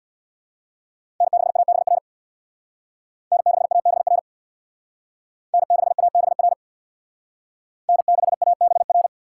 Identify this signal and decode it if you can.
Morse code, heard as a keyed tone, about 62 wpm, 700 Hz. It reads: N6ABR N6ABR N6ABR N6ABR
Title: Speed Racing - USA 1x3 Call Signs 50wpm